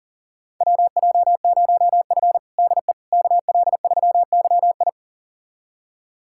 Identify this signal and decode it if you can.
Morse code, heard as a keyed tone, about 40 wpm, 700 Hz. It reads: WJ0F DE KL3YI
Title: USA Callsign DE Callsign 40wpm